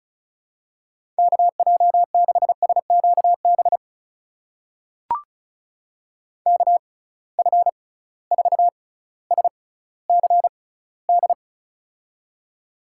Morse code, heard as a keyed tone, about 36 wpm, spaced as for 18 wpm, 700 Hz. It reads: KJ6SQB KF4SCD
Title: USA Call Signs - 6 Characters - 6x Character Spacing - 35wpm